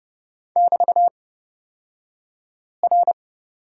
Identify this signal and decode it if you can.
Morse code, heard as a keyed tone, about 29 wpm, spaced as for 5 wpm, 700 Hz. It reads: <BT>R
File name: Lesson 36d - BT - Two Characters 30wpm